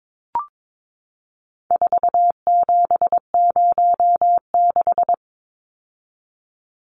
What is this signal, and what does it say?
Morse code, heard as a keyed tone, about 22 wpm, 700 Hz. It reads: 4706
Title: Numbers - 4-Digit 22wpm